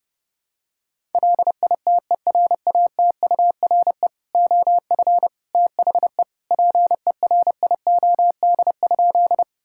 Morse code, heard as a keyed tone, about 32 wpm, 700 Hz. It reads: LITERATURE OF THE PERIOD?
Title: Sentences from Top 1000 Words - Review Part 5 - 30wpm